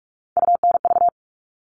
Morse code, heard as a keyed tone, about 44 wpm, 700 Hz. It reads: UNV
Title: All Cut Numbers - 3-Digit 45wpm